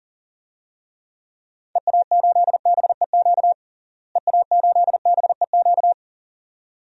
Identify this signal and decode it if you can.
Morse code, heard as a keyed tone, about 40 wpm, 700 Hz. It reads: EA8BEQ EA8BEQ
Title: DXCC-100 Call Signs - 6 Characters - Repeated 2x - 40wpm